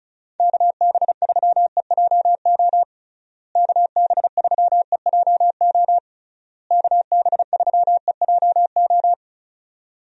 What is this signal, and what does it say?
Morse code, heard as a keyed tone, about 36 wpm, 700 Hz. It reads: KB3EJO KB3EJO KB3EJO
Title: USA Call Signs - 6 Characters - Repeated 3x - 35wpm